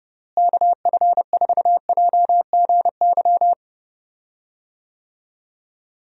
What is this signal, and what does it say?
Morse code, heard as a keyed tone, about 32 wpm, 700 Hz. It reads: KF4JGY
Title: USA Call Signs - 6 Characters - Repeated 2x - 30wpm